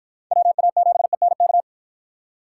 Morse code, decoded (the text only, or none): WA7ENK